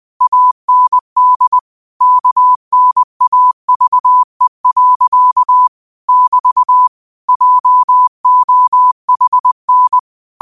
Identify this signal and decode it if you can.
Morse code, heard as a keyed tone, about 20 wpm, 1000 Hz. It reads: AND KNAVE. <BT> JOHN
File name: qotdfiles / 20wpm